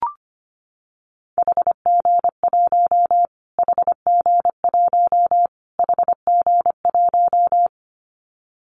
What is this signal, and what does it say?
Morse code, heard as a keyed tone, about 25 wpm, 700 Hz. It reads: HG1 HG1 HG1
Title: Callsign Prefixes for Top 25 DXCC - Repeated 3x - 25wpm